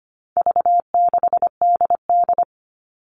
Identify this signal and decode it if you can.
Morse code, heard as a keyed tone, about 25 wpm, 700 Hz. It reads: V6DD
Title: All Cut Numbers - 4-Digit 25wpm